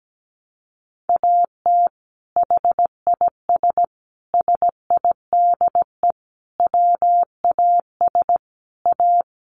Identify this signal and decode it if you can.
Morse code, heard as a keyed tone, about 17 wpm, 700 Hz. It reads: AT HIS SIDE WAS A
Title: Sentences from Top 500 Words 17wpm